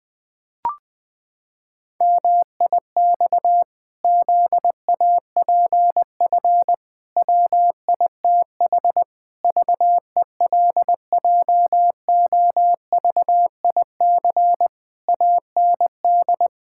Morse code, heard as a keyed tone, about 20 wpm, 700 Hz. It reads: MIX ZAPF WITH VELJOVIC AND
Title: Pangram Challenge v2 - 20wpm